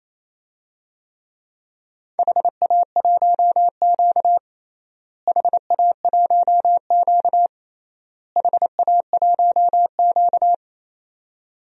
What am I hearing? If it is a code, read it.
Morse code, heard as a keyed tone, about 28 wpm, 700 Hz. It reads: HA1Q HA1Q HA1Q
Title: DXCC-100 Call Signs - 4 Characters - Repeated 3x - 28wpm